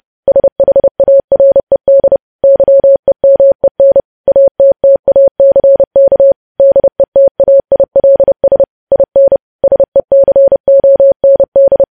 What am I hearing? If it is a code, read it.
Morse code, heard as a keyed tone, about 30 wpm, 550 Hz. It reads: SHARED YEMEN ATTACK DETAILS IN SECOND
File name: podcast / News headlines-1x-30x30